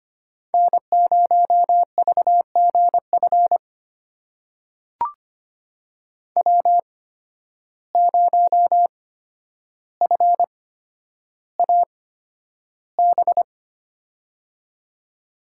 Morse code, heard as a keyed tone, about 25 wpm, spaced as for 11 wpm, 700 Hz. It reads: N0VGF W0FAB
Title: USA Call Signs - 5 Characters - 8x Character Spacing - 25wpm